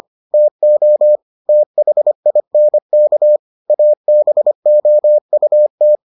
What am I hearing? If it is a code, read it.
Morse code, heard as a keyed tone, about 25 wpm, 600 Hz. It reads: TO THINK ABOUT